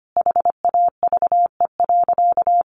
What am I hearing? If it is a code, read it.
Morse code, heard as a keyed tone, about 25 wpm, 700 Hz. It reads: HAVE.
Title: Sentences from Top 1200 Words 25wpm